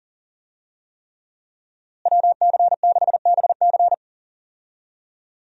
Morse code, heard as a keyed tone, about 40 wpm, 700 Hz. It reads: WC6BC
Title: USA Call Signs - 5 Characters - Repeated 3x - 40wpm